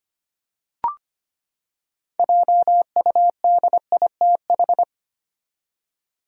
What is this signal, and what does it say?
Morse code, heard as a keyed tone, about 25 wpm, 700 Hz. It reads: JUDITH
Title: USA Common Names - 25wpm